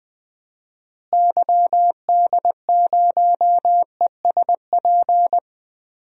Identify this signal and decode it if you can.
Morse code, heard as a keyed tone, about 20 wpm, 700 Hz. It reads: YD0ESP
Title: DXCC-100 Call Signs - 6 Characters - Repeated 2x - 20wpm